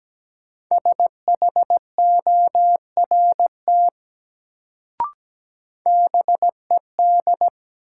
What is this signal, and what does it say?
Morse code, heard as a keyed tone, about 17 wpm, 700 Hz. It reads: SHORT BED
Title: Sets of 5 Words Encore - New Word Drill - 17wpm